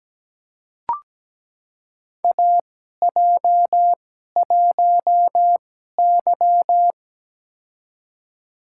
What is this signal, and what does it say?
Morse code, heard as a keyed tone, about 17 wpm, spaced as for 12 wpm, 700 Hz. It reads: AJ1Y
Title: USA Call Signs - 4 Characters - 2x Character Spacing - 17wpm